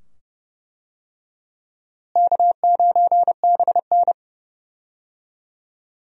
Morse code, heard as a keyed tone, about 32 wpm, 700 Hz. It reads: K9BN
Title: USA Call Signs - 4 Characters - Repeated 2x - 30wpm